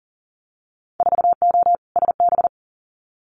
Morse code, heard as a keyed tone, about 40 wpm, 700 Hz. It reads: 4O SB
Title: Winter Field Day Exchanges 40wpm